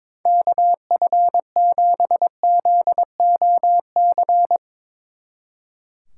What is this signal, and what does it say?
Morse code, heard as a keyed tone, about 22 wpm, 700 Hz. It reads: KF7ZOC
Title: Speed Racing - USA 2x3 Call Signs 22wpm